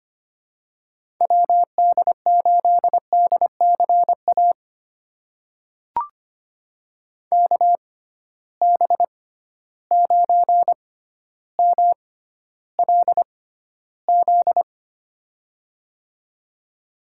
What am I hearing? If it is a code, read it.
Morse code, heard as a keyed tone, about 25 wpm, spaced as for 13 wpm, 700 Hz. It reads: WD8DCA KB9MLZ
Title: USA Call Signs - 6 Characters - 6x Character Spacing - 25wpm